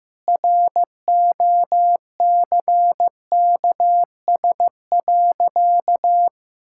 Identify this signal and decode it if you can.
Morse code, heard as a keyed tone, about 16 wpm, 700 Hz. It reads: ROCKS.
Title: Pangram Challenge v2 - 15wpm